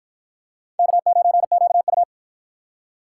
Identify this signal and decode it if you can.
Morse code, heard as a keyed tone, about 53 wpm, 700 Hz. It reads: K9QU